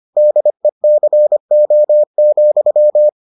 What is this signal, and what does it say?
Morse code, heard as a keyed tone, about 25 wpm, 600 Hz. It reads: DECO,